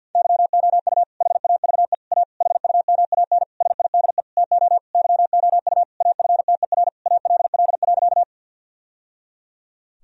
Morse code, heard as a keyed tone, about 50 wpm, 700 Hz. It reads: YOU HAVE A HUMAN SIDE TO YOU AFTER ALL.
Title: Sentences from Top 1000 Words - Review Part 3 - 50wpm